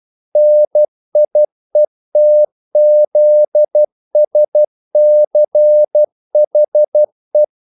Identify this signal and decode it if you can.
Morse code, heard as a keyed tone, about 12 wpm, 600 Hz. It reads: NIETZSCHE
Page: English quotations in CW (telegraphy - morse)